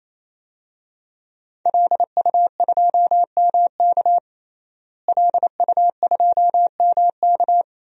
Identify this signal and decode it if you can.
Morse code, heard as a keyed tone, about 28 wpm, 700 Hz. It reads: LU2MK LU2MK
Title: DXCC-100 Call Signs - 5 Characters - Repeated 3x - 28wpm